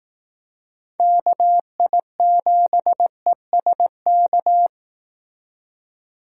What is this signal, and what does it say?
Morse code, heard as a keyed tone, about 18 wpm, 700 Hz. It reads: KI7ESK
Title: Speed Racing - USA 2x3 Call Signs 15wpm